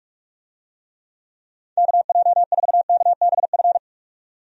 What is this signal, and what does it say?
Morse code, heard as a keyed tone, about 46 wpm, 700 Hz. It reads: KJ4KBF
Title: USA Call Signs - 6 Characters - 45wpm